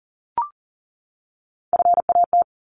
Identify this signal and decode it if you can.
Morse code, heard as a keyed tone, about 40 wpm, 700 Hz. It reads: FAT
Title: Top 501-750 Words 40wpm